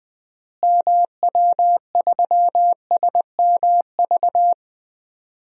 Morse code, heard as a keyed tone, about 20 wpm, 700 Hz. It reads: MW3SMV
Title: DXCC-100 Call Signs - 6 Characters - Repeated 2x - 20wpm